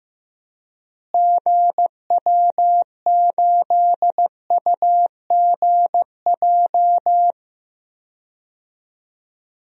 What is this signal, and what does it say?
Morse code, heard as a keyed tone, about 15 wpm, 700 Hz. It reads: GW8UGJ
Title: DXCC-100 Call Signs - 6 Characters - Repeated 3x - 15wpm